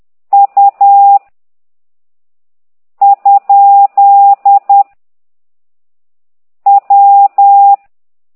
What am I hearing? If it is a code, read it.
Morse code, heard as a keyed tone, about 10 wpm, 800 Hz. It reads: U ? W
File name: ct2idw / exerc8